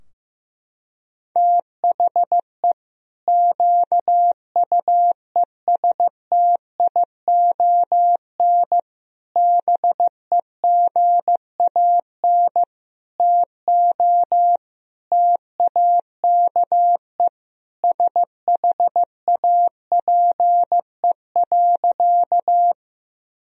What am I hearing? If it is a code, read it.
Morse code, heard as a keyed tone, about 16 wpm, 700 Hz. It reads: THE QUESTION BEGAN TO TAKE SHAPE.
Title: Sentences from Top 1000 Words - Review Part 3 - 15wpm